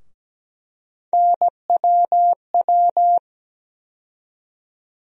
Morse code, heard as a keyed tone, about 17 wpm, 700 Hz. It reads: NWW
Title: USA Call Sign Postfixes - 3 Letters 17wpm